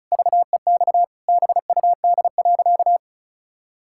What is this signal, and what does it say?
Morse code, heard as a keyed tone, about 36 wpm, 700 Hz. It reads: VEX BUD.
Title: Pangram Challenge v2 - 35wpm